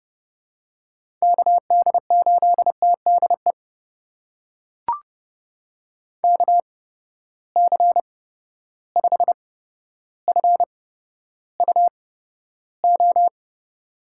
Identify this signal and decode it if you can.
Morse code, heard as a keyed tone, about 30 wpm, spaced as for 13 wpm, 700 Hz. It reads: KD8TDE KC5FUO
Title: USA Call Signs - 6 Characters - 8x Character Spacing - 30wpm